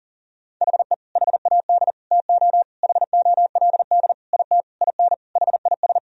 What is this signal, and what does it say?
Morse code, heard as a keyed tone, about 40 wpm, 700 Hz. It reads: HE HAD TO HOLD IT IN HIS
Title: Sentences from Top 1000 Words - Review Part 2 - 40wpm